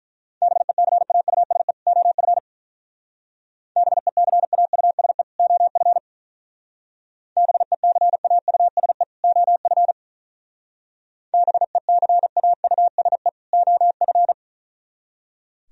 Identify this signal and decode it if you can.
Morse code, heard as a keyed tone, about 43 wpm, 700 Hz. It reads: BECAUSE OF BECAUSE OF BECAUSE OF BECAUSE OF
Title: Speed-Racing - Sets of 2 Words 35wpm